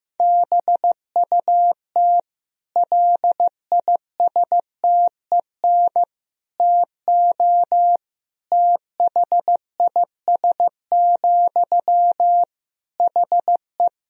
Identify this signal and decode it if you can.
Morse code, heard as a keyed tone, about 15 wpm, 700 Hz. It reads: BUT LISTEN TO THIS, HE
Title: Sentences from Top 400 Words and Abbreviations 15wpm